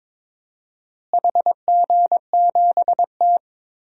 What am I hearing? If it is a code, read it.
Morse code, heard as a keyed tone, about 22 wpm, 700 Hz. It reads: HG7T
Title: DXCC-100 Call Signs - 4 Characters - Repeated 2x - 22wpm